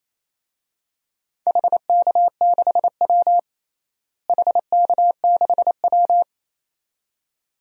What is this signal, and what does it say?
Morse code, heard as a keyed tone, about 28 wpm, 700 Hz. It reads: HK6W HK6W
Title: DXCC-100 Call Signs - 4 Characters - Repeated 2x - 28wpm